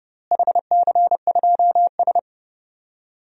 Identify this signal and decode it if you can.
Morse code, heard as a keyed tone, about 30 wpm, 700 Hz. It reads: HC2S
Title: DXCC-100 Call Signs - 4 Characters - Repeated 2x - 30wpm